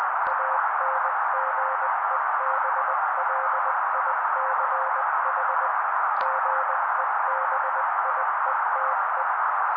Morse code, heard as a keyed tone, about 20 wpm, 530 Hz. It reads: ANGEBLICH GEBIETE
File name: daily / 18wpm noise